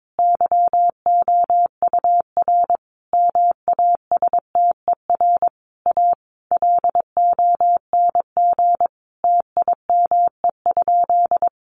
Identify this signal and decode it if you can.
Morse code, heard as a keyed tone, about 22 wpm, 700 Hz. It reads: YOUR MASTER A LONG TIME?
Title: Sentences from Top 1300 Words 22wpm